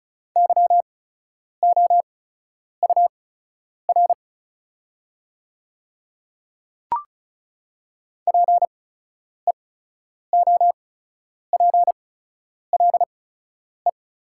Morse code, heard as a keyed tone, about 36 wpm, spaced as for 10 wpm, 690 Hz. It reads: YOUR PEOPLE
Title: 8x Character Spacing - Top 100 Words - 35wpm